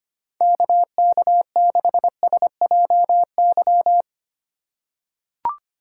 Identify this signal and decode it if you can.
Morse code, heard as a keyed tone, about 25 wpm, 700 Hz. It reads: KK6SJY
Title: USA Call Signs - 3x Character Spacing - 25wpm